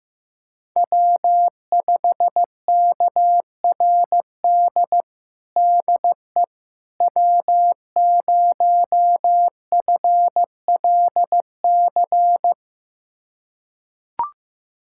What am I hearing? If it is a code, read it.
Morse code, heard as a keyed tone, about 15 wpm, 700 Hz. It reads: W5KRD DE W0FLC
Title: USA Callsign DE Callsign - 5 Characters - 15wpm